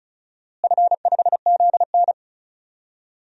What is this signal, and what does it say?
Morse code, heard as a keyed tone, about 34 wpm, 700 Hz. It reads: F5ZN
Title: DXCC-100 Call Signs - 4 Characters - Repeated 3x - 35wpm